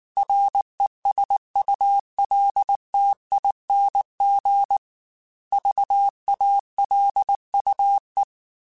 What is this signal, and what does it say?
Morse code, heard as a keyed tone, about 19 wpm, 800 Hz. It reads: RESULTING VALUE
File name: Technical Subjects / Technical Subjects-mp3s / XC @18